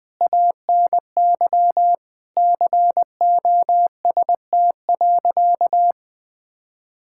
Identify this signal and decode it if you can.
Morse code, heard as a keyed tone, about 20 wpm, 700 Hz. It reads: ANY COST.
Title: Sentences from Top 1000 Words - Review Part 5 - 20wpm